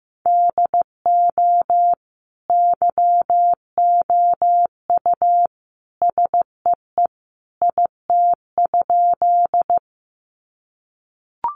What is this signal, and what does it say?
Morse code, heard as a keyed tone, about 15 wpm, 700 Hz. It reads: DO YOU SEE IT?
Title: Sentences from Top 100 Words 15wpm